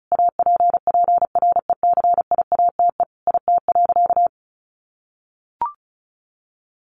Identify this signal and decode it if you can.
Morse code, heard as a keyed tone, about 34 wpm, 700 Hz. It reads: APPRECIATE IT.
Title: Sentences from Top 1300 Words 35wpm